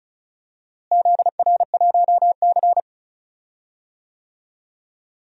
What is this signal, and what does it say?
Morse code, heard as a keyed tone, about 34 wpm, 700 Hz. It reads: ZR1C
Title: DXCC-100 Call Signs - 4 Characters - Repeated 3x - 35wpm